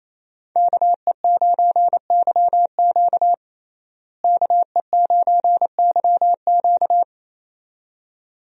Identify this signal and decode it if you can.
Morse code, heard as a keyed tone, about 28 wpm, 700 Hz. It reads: KE9YQ KE9YQ
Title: USA Call Signs - 5 Characters - Repeated 2x - 28wpm